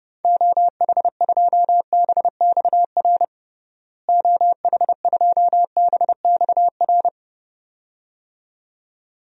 Morse code, heard as a keyed tone, about 32 wpm, 700 Hz. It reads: OH2BXR OH2BXR
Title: DXCC-100 Call Signs - 6 Characters - Repeated 2x - 30wpm